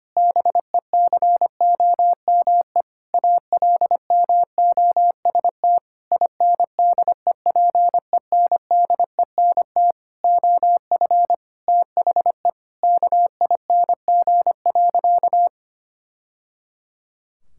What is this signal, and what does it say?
Morse code, heard as a keyed tone, about 25 wpm, 700 Hz. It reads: BECOME ALMOST INDEPENDENT OF THE KING.
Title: Sentences from Top 1000 Words - Review Part 6 - 25wpm